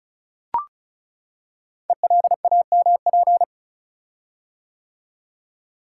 Morse code, heard as a keyed tone, about 36 wpm, 690 Hz. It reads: ELAMP
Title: Code Groups - 5 Letters - 35wpm